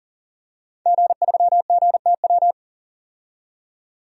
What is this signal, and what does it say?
Morse code, heard as a keyed tone, about 39 wpm, 700 Hz. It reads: G3GTW
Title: DXCC-100 Call Signs - 5 Characters - 40wpm